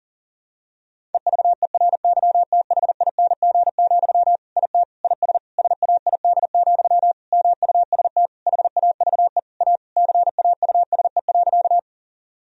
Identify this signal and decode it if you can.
Morse code, heard as a keyed tone, about 40 wpm, 700 Hz. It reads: EVERYTHING, IT IS SAID, MUST HAVE A CAUSE.
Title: Sentences from Top 1000 Words - Review Part 1 - 40wpm